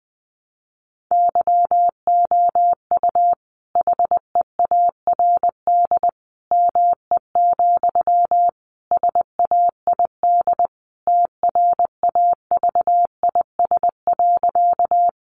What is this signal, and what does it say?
Morse code, heard as a keyed tone, about 20 wpm, 700 Hz. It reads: YOU HEARD ME, SAID TRAVIS.
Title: A Sound Of Thunder 20wpm